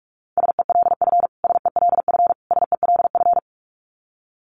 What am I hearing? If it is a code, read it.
Morse code, heard as a keyed tone, about 44 wpm, 700 Hz. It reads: SELF SELF SELF
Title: Top 500 Words Repeated 45wpm